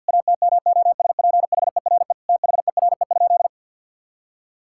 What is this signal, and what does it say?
Morse code, heard as a keyed tone, about 50 wpm, 700 Hz. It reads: ATMOSPHERE THERE?
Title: Sentences from Top 1600 Words 50wpm